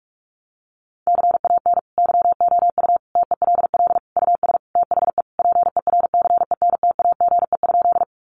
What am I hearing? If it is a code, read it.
Morse code, heard as a keyed tone, about 46 wpm, 700 Hz. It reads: CAN YOU TELL US THE PERCENTAGE?
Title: Sentences from Top 1300 Words 45wpm